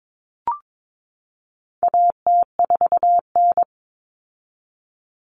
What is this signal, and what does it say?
Morse code, heard as a keyed tone, about 22 wpm, 700 Hz. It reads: AT4N
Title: Common Cut Numbers - 4-Digit 22wpm